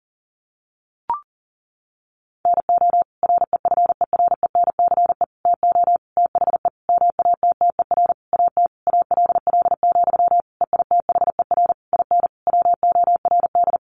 Morse code, heard as a keyed tone, about 40 wpm, 700 Hz. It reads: NO REFERENCE TO THE MATTER AT ALL, EITHER IN WORD
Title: Sentences from Top 1200 Words 40wpm